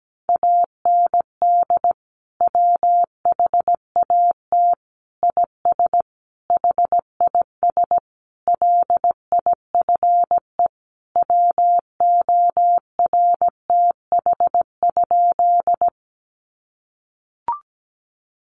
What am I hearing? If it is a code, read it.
Morse code, heard as a keyed tone, about 17 wpm, 700 Hz. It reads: AND WHAT IS HIS LIFE WORTH?
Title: Sentences from Top 800 Words 17wpm